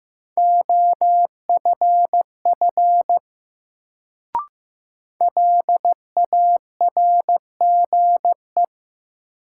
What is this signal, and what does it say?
Morse code, heard as a keyed tone, about 15 wpm, 700 Hz. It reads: OFF LARGE
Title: Sentences from Top 200 Words - New Word Drill - 15wpm